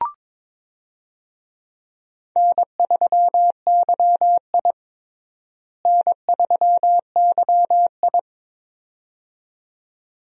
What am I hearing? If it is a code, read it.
Morse code, heard as a keyed tone, about 22 wpm, 700 Hz. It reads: N3YI N3YI
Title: USA Call Signs - 4 Characters - Repeated 2x - 22wpm